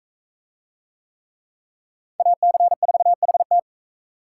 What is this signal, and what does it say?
Morse code, heard as a keyed tone, about 38 wpm, 700 Hz. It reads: AC4HT